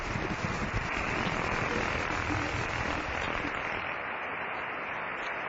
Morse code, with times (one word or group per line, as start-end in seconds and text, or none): none